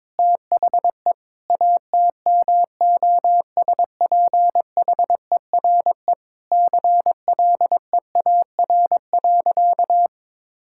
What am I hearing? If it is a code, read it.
Morse code, heard as a keyed tone, about 22 wpm, 700 Hz. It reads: THE ATMOSPHERE CLEAR.
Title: Sentences from Top 1600 Words 22wpm